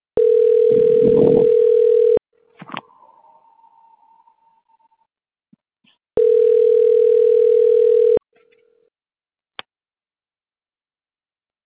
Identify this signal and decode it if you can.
Ringback tone (North America), 440 + 480 Hz.
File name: recordings / RE0f9b b3ae5b e0d1a5